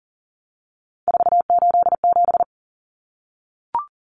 Morse code, heard as a keyed tone, about 40 wpm, 700 Hz. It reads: 487
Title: Numbers - 3-Digit 40wpm